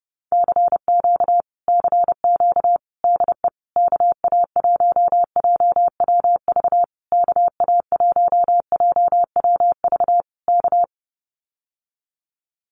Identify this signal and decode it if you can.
Morse code, heard as a keyed tone, about 30 wpm, 700 Hz. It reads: CQ CQ DE KA1JWV KA1JWV K
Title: Common QSO Phrases Encore 30wpm